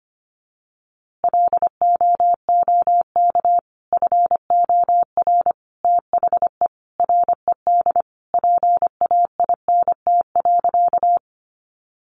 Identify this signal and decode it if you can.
Morse code, heard as a keyed tone, about 25 wpm, 700 Hz. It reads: LOOK FOR THE RED PAINT.
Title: A Sound Of Thunder 25wpm